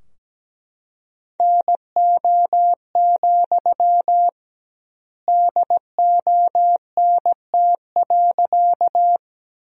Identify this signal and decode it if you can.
Morse code, heard as a keyed tone, about 17 wpm, 700 Hz. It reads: NO, DONT.
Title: 2x Word-Spacing - Sentences from Top 300 Words - 17wpm